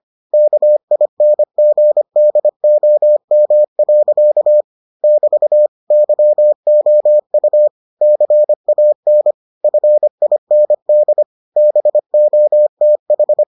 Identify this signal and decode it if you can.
Morse code, heard as a keyed tone, about 25 wpm, 600 Hz. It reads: KINGDOM. <BT> YOU CAN FIND BOTH